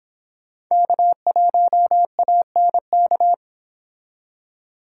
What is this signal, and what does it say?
Morse code, heard as a keyed tone, about 23 wpm, 700 Hz. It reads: K1ANK